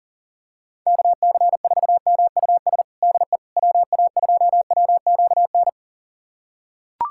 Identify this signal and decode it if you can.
Morse code, heard as a keyed tone, about 40 wpm, 700 Hz. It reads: KC4MUS DE WA2WQN
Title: USA Callsign DE Callsign 40wpm